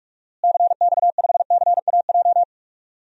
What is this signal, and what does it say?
Morse code, heard as a keyed tone, about 41 wpm, 700 Hz. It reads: CX5CAJ
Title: DXCC-100 Call Signs - 6 Characters - Repeated 2x - 45wpm